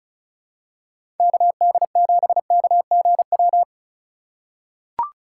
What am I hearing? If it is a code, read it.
Morse code, heard as a keyed tone, about 36 wpm, 700 Hz. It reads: KD7KGW
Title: USA Call Signs - 6 Characters - 2x Character Spacing - 35wpm